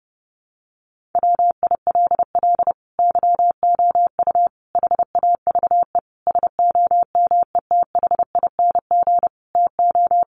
Morse code, heard as a keyed tone, about 30 wpm, 700 Hz. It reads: WILL YOU HAVE SOMETHING TO
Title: Sentences from Top 600 Words 30wpm